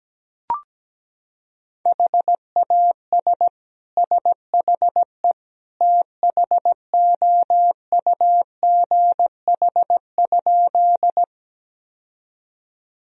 Morse code, heard as a keyed tone, about 17 wpm, 700 Hz. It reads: HAS SHE THOUGH?
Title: Sentences from Top 1000 Words - Review Part 2 - 17wpm